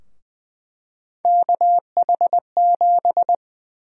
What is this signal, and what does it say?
Morse code, heard as a keyed tone, about 20 wpm, 700 Hz. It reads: KH7
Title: USA Call Sign Prefixes - Repeated 2x - 20wpm